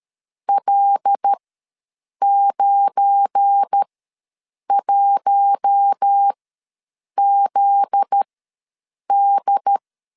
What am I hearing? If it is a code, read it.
Morse code, heard as a keyed tone, about 13 wpm, spaced as for 7 wpm, 800 Hz. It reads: L91ZD
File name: LEZIONI / 8 YJ45 / 8 YJ4 ws 12wpm 3249